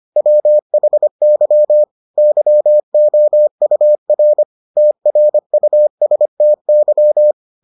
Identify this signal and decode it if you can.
Morse code, heard as a keyed tone, about 25 wpm, 600 Hz. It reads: WHY YOUR TRUSTY